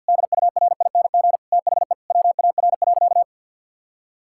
Morse code, heard as a keyed tone, about 50 wpm, 700 Hz. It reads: DURING THE WAR.
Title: Sentences from Top 1000 Words - Review Part 6 - 50wpm